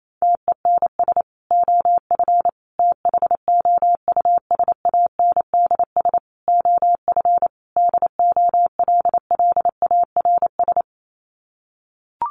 Morse code, heard as a keyed tone, about 28 wpm, 700 Hz. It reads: TENS OF THOUSANDS OF DOLLARS
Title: Sets of 5 Words Encore2 28wpm